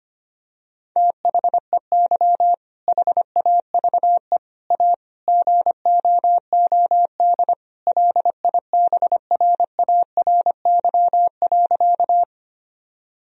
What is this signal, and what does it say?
Morse code, heard as a keyed tone, about 25 wpm, 700 Hz. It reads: THEY HAVE A GOOD LIBRARY.
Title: Sentences from Top 1000 Words - Review Part 4 - 25wpm